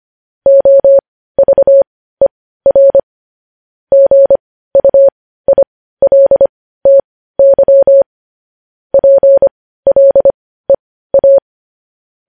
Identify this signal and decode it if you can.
Morse code, heard as a keyed tone, about 25 wpm, spaced as for 15 wpm, 550 Hz. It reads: OVER GUILTY PLEA
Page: news-headlines-in-morse-code-at-15-wpm